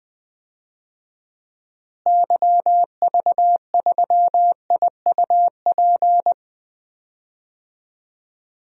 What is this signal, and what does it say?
Morse code, heard as a keyed tone, about 21 wpm, 700 Hz. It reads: YV3IUP
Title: DXCC-100 Call Signs - 6 Characters - Repeated 3x - 20wpm